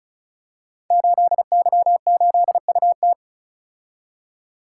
Morse code, heard as a keyed tone, about 36 wpm, 700 Hz. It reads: 8Y8UT
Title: Code Groups - 5 Letter:Numbers - 35wpm